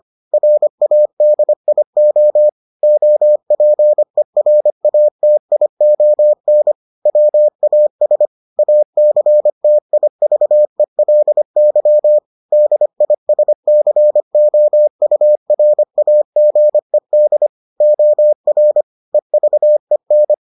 Morse code, heard as a keyed tone, about 25 wpm, 600 Hz. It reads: RADIO OPERATION WAS ACTIVELY DISCOURAGED OR EVEN